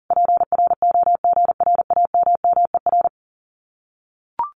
Morse code, heard as a keyed tone, about 40 wpm, 700 Hz. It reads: PROGRAMMER
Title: Common QSO Phrases Encore 40wpm